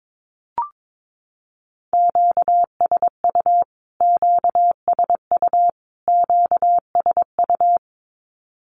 Morse code, heard as a keyed tone, about 22 wpm, 700 Hz. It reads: QSU QSU QSU
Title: Q-Codes Repeated 3x 22wpm